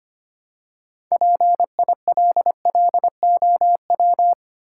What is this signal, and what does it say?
Morse code, heard as a keyed tone, about 25 wpm, 700 Hz. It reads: PILLOW
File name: practice / 25wpm / Easy Words with Hints - 25wpm